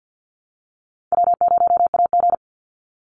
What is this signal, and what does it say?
Morse code, heard as a keyed tone, about 52 wpm, 700 Hz. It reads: W0AG
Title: Easy Call Signs - 50wpm